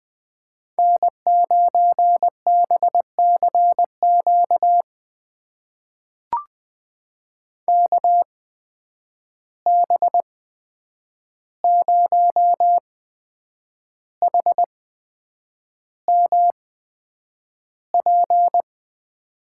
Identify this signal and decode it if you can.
Morse code, heard as a keyed tone, about 20 wpm, spaced as for 8 wpm, 700 Hz. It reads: N9BCQ KB0HMP
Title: USA Call Signs - 8x Character Spacing - 20wpm